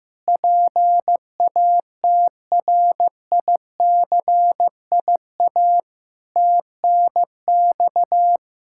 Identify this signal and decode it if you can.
Morse code, heard as a keyed tone, about 15 wpm, 700 Hz. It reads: PATRICIA TNX
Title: Practice QSO - Exchange 3 - More Info - 15wpm